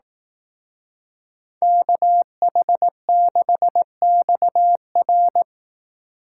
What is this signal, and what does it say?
Morse code, heard as a keyed tone, about 18 wpm, 700 Hz. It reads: KH6XR
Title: Speed Racing - USA 2x2 Call Signs 15wpm